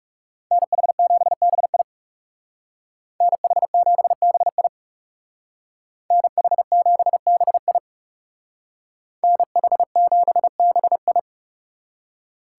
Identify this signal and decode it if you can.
Morse code, heard as a keyed tone, about 40 wpm, 700 Hz. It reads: NH7BI NH7BI NH7BI NH7BI